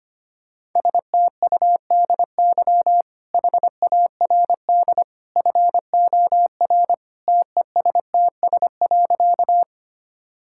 Morse code, heard as a keyed tone, about 25 wpm, 690 Hz. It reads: STUDY HARD FOR TESTS.
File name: Easy Sentences with Hints - 25wpm